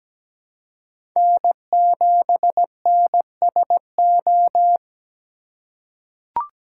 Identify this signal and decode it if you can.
Morse code, heard as a keyed tone, about 17 wpm, 700 Hz. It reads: N7NSO
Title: USA Call Signs - 5 Characters - 2x Character Spacing - 17wpm